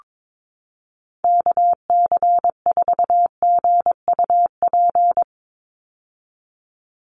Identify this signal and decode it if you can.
Morse code, heard as a keyed tone, about 22 wpm, 700 Hz. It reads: KC4GUP
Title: USA Call Signs 22wpm